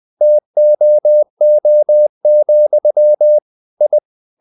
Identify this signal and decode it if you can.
Morse code, heard as a keyed tone, about 20 wpm, 600 Hz. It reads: TOO, I
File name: the old man and the sea / the old man and the sea part 1 20wpm